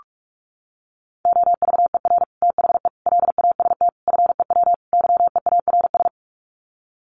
Morse code, heard as a keyed tone, about 44 wpm, 700 Hz. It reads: OVER THE LAST FEW YEARS
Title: Sets of 5 Words Encore 45wpm